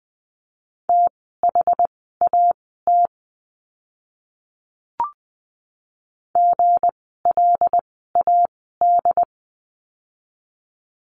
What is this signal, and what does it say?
Morse code, heard as a keyed tone, about 20 wpm, spaced as for 14 wpm, 700 Hz. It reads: THAT GLAD
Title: 160 Lesson 26f - Letter g - Words 20wpm